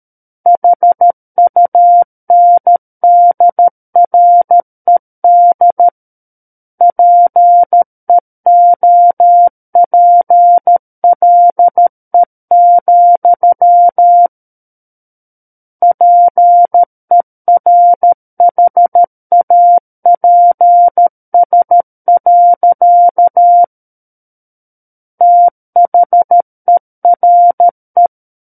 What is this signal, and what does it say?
Morse code, heard as a keyed tone, about 13 wpm, 700 Hz. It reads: HUNDRED PEOPLE, PERHAPS. THERE
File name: Book One / 04-Chapter 4 - 13 WPM